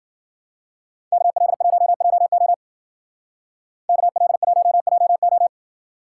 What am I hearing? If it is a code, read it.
Morse code, heard as a keyed tone, about 50 wpm, 700 Hz. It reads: KL1JO KL1JO
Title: Speed Racing - USA 2x2 Call Signs 40wpm